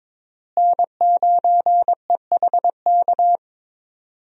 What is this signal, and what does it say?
Morse code, heard as a keyed tone, about 22 wpm, 700 Hz. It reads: N9EHK
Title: USA Call Signs - 5 Characters - Repeated 3x - 22wpm